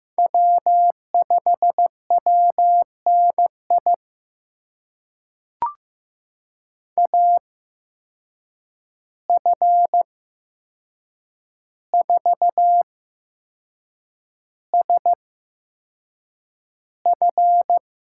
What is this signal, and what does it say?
Morse code, heard as a keyed tone, about 15 wpm, spaced as for 6 wpm, 700 Hz. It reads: W5WNI AF4SF
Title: USA Call Signs - 5 Characters - 8x Character Spacing - 15wpm